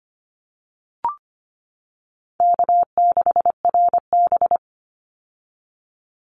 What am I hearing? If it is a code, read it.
Morse code, heard as a keyed tone, about 25 wpm, 700 Hz. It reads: K6RB
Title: Easy Call Signs - 25wpm